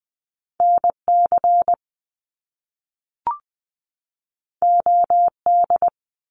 Morse code, heard as a keyed tone, about 20 wpm, 700 Hz. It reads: NC OD
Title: Top Two Letter Combinations in Words 20wpm